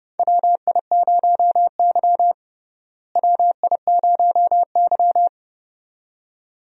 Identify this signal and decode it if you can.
Morse code, heard as a keyed tone, about 31 wpm, 700 Hz. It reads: WI0Y WI0Y
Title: USA Call Signs - 4 Characters - Repeated 2x - 30wpm